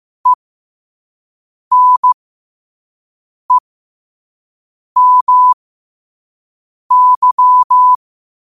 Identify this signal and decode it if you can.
Morse code, heard as a keyed tone, about 15 wpm, spaced as for 5 wpm, 1000 Hz. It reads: ENEMY
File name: qotdfiles / 5wpm